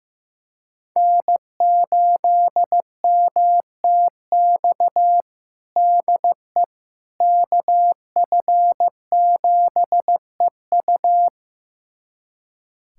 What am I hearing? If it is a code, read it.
Morse code, heard as a keyed tone, about 16 wpm, 700 Hz. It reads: N8MTX DE KF7EU
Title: USA Callsign DE Callsign - 5 Characters - 15wpm